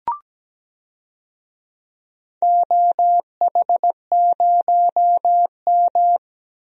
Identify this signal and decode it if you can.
Morse code, heard as a keyed tone, about 17 wpm, 700 Hz. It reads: OH0M
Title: DXCC-100 Call Signs - 4 Characters - Repeated 3x - 17wpm